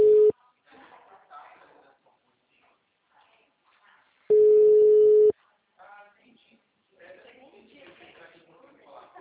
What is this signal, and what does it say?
Ringback tone (Europe, 425 Hz), with speech.